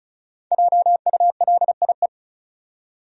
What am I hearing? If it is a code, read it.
Morse code, heard as a keyed tone, about 36 wpm, 700 Hz. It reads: JULIE
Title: USA Common Names - Max 5 Letters - 35wpm